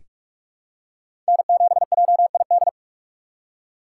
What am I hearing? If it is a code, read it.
Morse code, heard as a keyed tone, about 46 wpm, 700 Hz. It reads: N7JID
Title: USA Call Signs - 2x Character Spacing - 45wpm